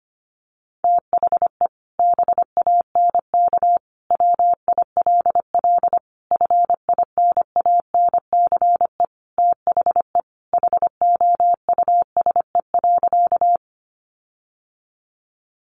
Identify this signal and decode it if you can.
Morse code, heard as a keyed tone, about 25 wpm, 700 Hz. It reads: THE BANK WILL FINANCE THE HOUSE.
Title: Sentences from Top 1500 Words 25wpm